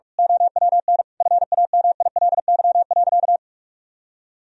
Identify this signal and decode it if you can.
Morse code, heard as a keyed tone, about 46 wpm, 700 Hz. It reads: OWN FAMILY.
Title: Sentences from Top 1000 Words - Review Part 1 - 45wpm